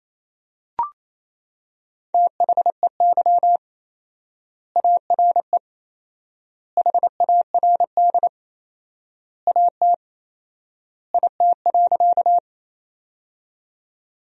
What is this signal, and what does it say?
Morse code, heard as a keyed tone, about 28 wpm, 700 Hz. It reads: THEY ARE HARD AT IT.
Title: 4x Word-Spacing - Sentences from Top 300 Words - 28wpm